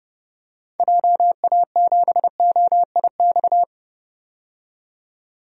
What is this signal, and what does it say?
Morse code, heard as a keyed tone, about 30 wpm, 700 Hz. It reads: JA7OIX
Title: DXCC-100 Call Signs - 6 Characters - 30wpm